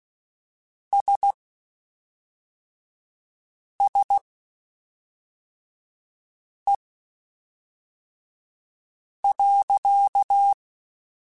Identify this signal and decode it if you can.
Morse code, heard as a keyed tone, about 16 wpm, spaced as for 3 wpm, 800 Hz. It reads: SSE.
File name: Spaced Characters / Spaced Characters-mp3s / Weather4 14 sp @5